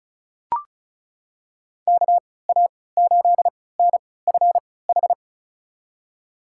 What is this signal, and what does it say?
Morse code, heard as a keyed tone, about 34 wpm, spaced as for 20 wpm, 700 Hz. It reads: KA8NFH
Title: USA Call Signs - 6 Characters - 3x Character Spacing - 35wpm